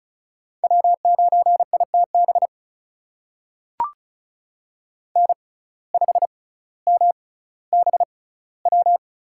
Morse code, heard as a keyed tone, about 36 wpm, spaced as for 18 wpm, 700 Hz. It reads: W9ITB N5MBW
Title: USA Call Signs - 5 Characters - 6x Character Spacing - 35wpm